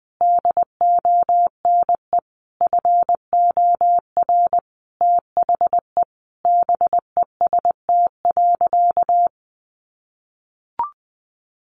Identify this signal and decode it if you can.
Morse code, heard as a keyed tone, about 20 wpm, 700 Hz. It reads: DONE FOR THE BEST.
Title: Sentences from Top 600 Words 20wpm